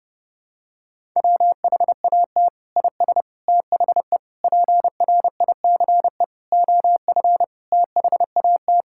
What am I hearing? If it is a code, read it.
Morse code, heard as a keyed tone, about 32 wpm, 700 Hz. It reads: WHAT IS THE PRICE OF THAT
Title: Sentences from Top 1000 Words - Review Part 5 - 30wpm